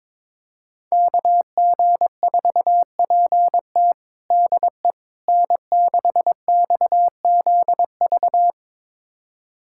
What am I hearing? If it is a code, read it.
Morse code, heard as a keyed tone, about 22 wpm, 700 Hz. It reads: KG4PT DE N6XZV
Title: USA Callsign DE Callsign - 5 Characters - 22wpm